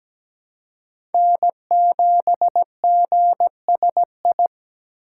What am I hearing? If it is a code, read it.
Morse code, heard as a keyed tone, about 17 wpm, 700 Hz. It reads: N7GSI
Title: USA Call Signs - 5 Characters - 4x Character Spacing - 17wpm